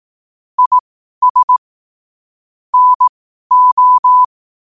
Morse code, heard as a keyed tone, about 18 wpm, spaced as for 12 wpm, 1000 Hz. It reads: IS NO
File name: qotdfiles / 12wpm